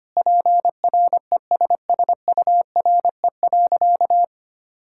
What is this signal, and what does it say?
Morse code, heard as a keyed tone, about 25 wpm, 700 Hz. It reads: PRESSURE.
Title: Sentences from Top 1000 Words - Review Part 6 - 25wpm